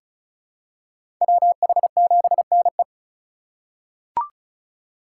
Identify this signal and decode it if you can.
Morse code, heard as a keyed tone, about 34 wpm, 700 Hz. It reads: WH7NE
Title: USA Call Signs - 5 Characters - 4x Character Spacing - 35wpm